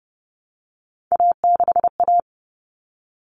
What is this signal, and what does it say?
Morse code, heard as a keyed tone, about 30 wpm, 700 Hz. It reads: A6A
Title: Common Cut Numbers - 3-Digit 30wpm